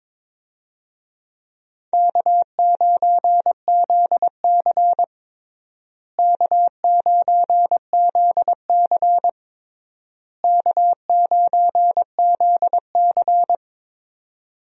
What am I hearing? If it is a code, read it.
Morse code, heard as a keyed tone, about 22 wpm, 700 Hz. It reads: K9ZC K9ZC K9ZC
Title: USA Call Signs - 4 Characters - Repeated 3x - 22wpm